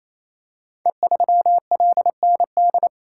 Morse code, heard as a keyed tone, about 27 wpm, 700 Hz. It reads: E3LND
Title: Code Groups - 5 Letter:Numbers - 28wpm